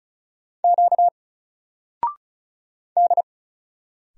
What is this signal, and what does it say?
Morse code, heard as a keyed tone, about 36 wpm, 700 Hz. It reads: Q D
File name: Single-Letter-Number-35wpm